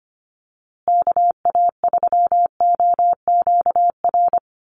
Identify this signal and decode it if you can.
Morse code, heard as a keyed tone, about 25 wpm, 700 Hz. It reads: KA3OQR
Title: USA Call Signs Repeated 3x 25wpm